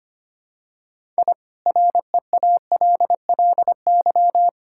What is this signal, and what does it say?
Morse code, heard as a keyed tone, about 25 wpm, 700 Hz. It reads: I REALLY
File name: practice / 25wpm / Sentences with 4-Word Phrases - 25wpm